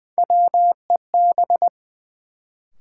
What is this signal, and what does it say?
Morse code, heard as a keyed tone, about 20 wpm, 700 Hz. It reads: WEB
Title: Sentences from Top 1000 Words - Review Part 4 - New Word Drill - 20wpm